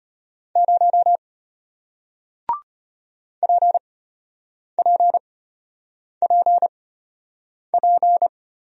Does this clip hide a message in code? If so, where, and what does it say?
Yes: Morse code from 0.6 to 8.3 s, heard as a keyed tone, about 33 wpm, spaced as for 8 wpm, 700 Hz. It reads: 0 PPPP